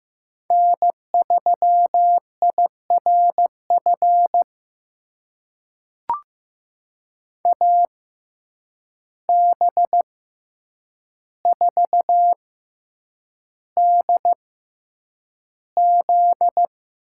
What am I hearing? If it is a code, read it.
Morse code, heard as a keyed tone, about 15 wpm, spaced as for 8 wpm, 700 Hz. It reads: N3IRF AB4DZ
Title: USA Call Signs - 5 Characters - 6x Character Spacing - 15wpm